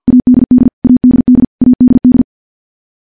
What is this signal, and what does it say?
Morse code, heard as a keyed tone, about 46 wpm, 250 Hz. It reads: ADD ADD ADD
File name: morse / morse-training-50WPM